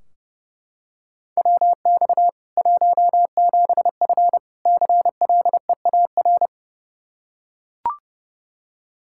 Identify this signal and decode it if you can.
Morse code, heard as a keyed tone, about 30 wpm, 700 Hz. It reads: WX 17F CLEAR
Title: Practice QSO - Exchange 2 - Info - 30wpm